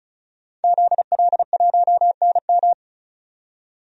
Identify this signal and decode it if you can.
Morse code, heard as a keyed tone, about 36 wpm, 700 Hz. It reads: ZL1NM
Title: DXCC-100 Call Signs - 5 Characters - 35wpm